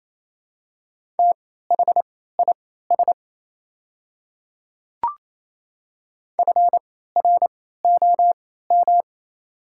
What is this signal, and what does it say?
Morse code, heard as a keyed tone, about 28 wpm, spaced as for 16 wpm, 700 Hz. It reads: THIS FROM
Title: 3x Character Spacing - Top 100 Words - 28wpm